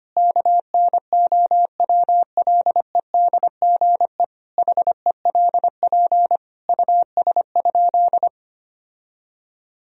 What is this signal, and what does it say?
Morse code, heard as a keyed tone, about 25 wpm, 700 Hz. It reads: KNOWLEDGE HELP US?
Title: Sentences from Top 1000 Words - Review Part 3 - 25wpm